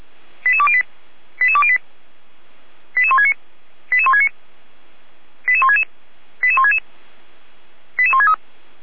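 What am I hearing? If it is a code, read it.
Five-tone selective call (ZVEI), heard as quick runs of five tones, tones of about 70 ms. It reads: 88208 88208 88169 88169 88166 88166 88163
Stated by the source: ZVEI five-tone selective call received by SDR.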